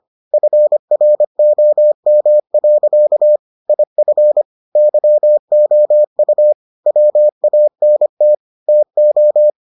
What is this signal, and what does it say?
Morse code, heard as a keyed tone, about 25 wpm, 600 Hz. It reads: FROM. IF YOU WANT TO